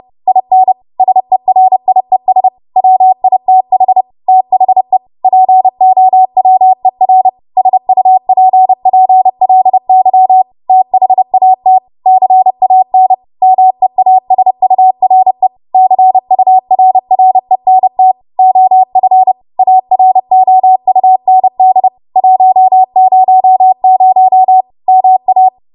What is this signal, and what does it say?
Morse code, heard as a keyed tone, about 30 wpm, 750 Hz. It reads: IN SERIES WITH THE POWER SUPPLY THAT CAN MEASURE CURRENT OF AROUND 100 MA
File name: Archive / 30 WPM / 30WPM